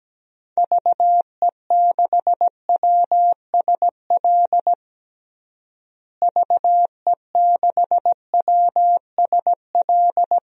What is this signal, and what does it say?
Morse code, heard as a keyed tone, about 17 wpm, 700 Hz. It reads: VE6WSL VE6WSL
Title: DXCC-100 Call Signs - 6 Characters - Repeated 2x - 17wpm